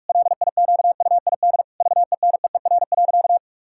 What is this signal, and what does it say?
Morse code, heard as a keyed tone, about 46 wpm, 700 Hz. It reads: LIQUID VENEER.
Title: Pangram Challenge v2 - 45wpm